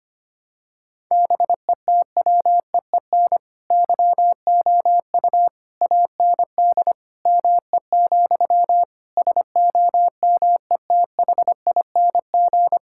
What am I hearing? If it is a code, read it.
Morse code, heard as a keyed tone, about 25 wpm, 700 Hz. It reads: BETWEEN YOU AND ME, SOMETHING
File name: practice / 25wpm / All-Age Appropriate Jokes - 25wpm